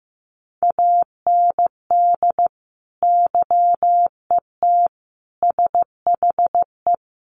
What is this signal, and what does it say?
Morse code, heard as a keyed tone, about 15 wpm, 700 Hz. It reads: AND YET SHE
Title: Sentences from Top 1500 Words 15wpm